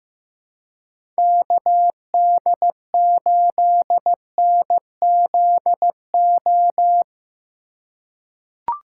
Morse code, heard as a keyed tone, about 15 wpm, 700 Hz. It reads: KD8NZO
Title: USA Call Signs - 6 Characters - 4x Character Spacing - 15wpm